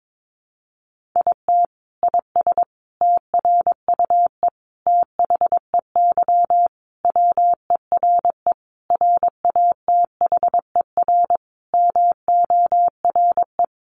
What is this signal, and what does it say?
Morse code, heard as a keyed tone, about 22 wpm, 700 Hz. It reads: IT IS TRUE THEY WERE RATHER MORE
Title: Sentences from Top 1400 Words 22wpm